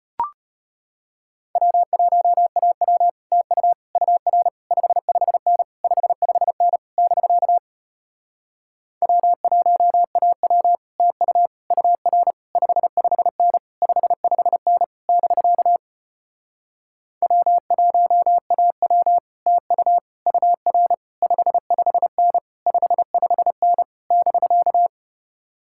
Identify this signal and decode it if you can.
Morse code, heard as a keyed tone, about 33 wpm, 700 Hz. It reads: W1AW TU UR 55N 55N <BK> W1AW TU UR 55N 55N <BK> W1AW TU UR 55N 55N <BK>